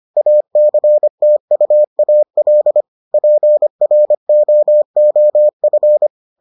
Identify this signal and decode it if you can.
Morse code, heard as a keyed tone, about 25 wpm, 600 Hz. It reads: ACTUAL PROOF